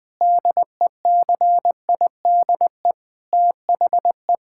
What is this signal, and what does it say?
Morse code, heard as a keyed tone, about 20 wpm, 700 Hz. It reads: DECIDE THE
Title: Sentences from Top 1600 Words 20wpm